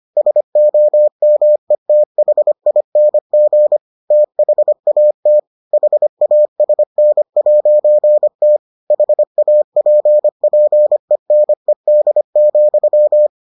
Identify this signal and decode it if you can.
Morse code, heard as a keyed tone, about 25 wpm, 600 Hz. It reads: SOMETHING THAT HASN'T HAPPENED,